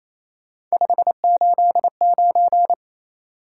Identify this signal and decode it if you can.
Morse code, heard as a keyed tone, about 36 wpm, 700 Hz. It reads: 589